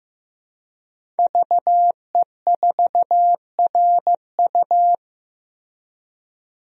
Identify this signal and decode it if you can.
Morse code, heard as a keyed tone, about 15 wpm, 700 Hz. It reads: VE4RU
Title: DXCC-100 Call Signs - 5 Characters - Repeated 3x - 15wpm